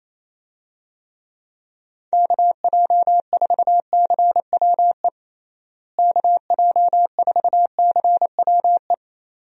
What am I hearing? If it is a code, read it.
Morse code, heard as a keyed tone, about 28 wpm, 700 Hz. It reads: KJ4CWE KJ4CWE
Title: USA Call Signs - 6 Characters - Repeated 2x - 28wpm